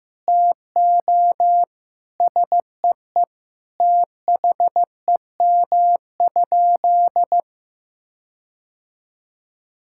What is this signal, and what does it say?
Morse code, heard as a keyed tone, about 15 wpm, 700 Hz. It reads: TO SEE THEM?
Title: Sentences from Top 1600 Words 15wpm